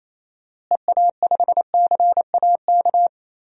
Morse code, heard as a keyed tone, about 27 wpm, 700 Hz. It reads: EA5CAK
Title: DXCC-100 Call Signs - 6 Characters - Repeated 3x - 28wpm